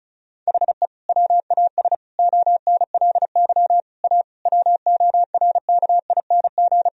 Morse code, heard as a keyed tone, about 36 wpm, 700 Hz. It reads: HE WAS ONLY A WORKING
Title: Sentences from Top 1000 Words - Review Part 4 - 35wpm